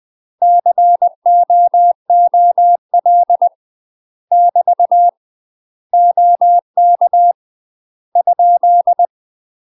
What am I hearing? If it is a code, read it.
Morse code, heard as a keyed tone, about 20 wpm, 700 Hz. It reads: COOL <BT> OK ?
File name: UK MASTER QSOs TRAINING / 20WPM / UK MASTER QSOs 0011